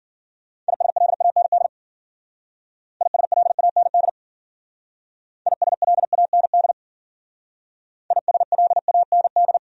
Morse code, heard as a keyed tone, about 51 wpm, 700 Hz. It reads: ISLAND ISLAND ISLAND ISLAND